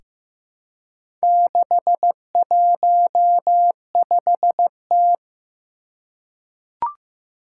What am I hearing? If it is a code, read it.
Morse code, heard as a keyed tone, about 15 wpm, 700 Hz. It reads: 615T
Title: Four Challenging Letter-Number Combinations 15wpm